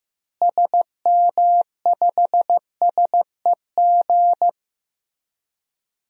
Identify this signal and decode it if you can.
Morse code, heard as a keyed tone, about 15 wpm, 700 Hz. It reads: SM5SEG
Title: DXCC-100 Call Signs - 6 Characters - Repeated 3x - 15wpm